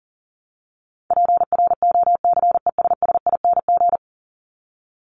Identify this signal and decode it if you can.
Morse code, heard as a keyed tone, about 40 wpm, 700 Hz. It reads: PROCESSING
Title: ING Suffix - Top 100 Words with Base Words 40wpm